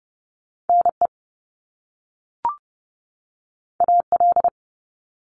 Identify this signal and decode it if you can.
Morse code, heard as a keyed tone, about 30 wpm, 700 Hz. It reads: NE AL
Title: US States 30wpm